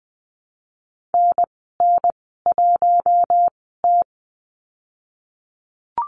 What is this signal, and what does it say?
Morse code, heard as a keyed tone, about 20 wpm, spaced as for 15 wpm, 700 Hz. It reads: NN1T
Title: 048 Lesson 9g - Number 4 - Call Signs 20wpm